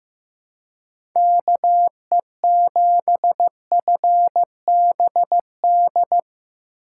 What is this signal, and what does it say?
Morse code, heard as a keyed tone, about 17 wpm, 700 Hz. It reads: KE7FBD
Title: Speed Racing - USA 2x3 Call Signs 15wpm